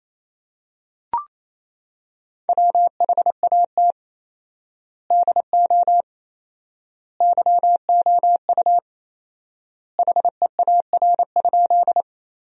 Morse code, heard as a keyed tone, about 28 wpm, 700 Hz. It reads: WHAT DO YOU HEAR?
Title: 4x Word-Spacing - Sentences from Top 300 Words - 28wpm